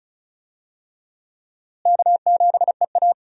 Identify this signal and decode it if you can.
Morse code, heard as a keyed tone, about 36 wpm, 700 Hz. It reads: K7EA
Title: USA Call Signs - 4 Characters - Repeated 3x - 35wpm